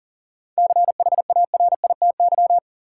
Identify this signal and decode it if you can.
Morse code, heard as a keyed tone, about 39 wpm, 700 Hz. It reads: CHARITY
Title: Seven Letter Words - 40wpm